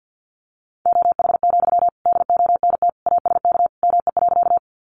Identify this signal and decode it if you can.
Morse code, heard as a keyed tone, about 50 wpm, 700 Hz. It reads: OH, DONT ASK ME.
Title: Sentences from Top 500 Words 50wpm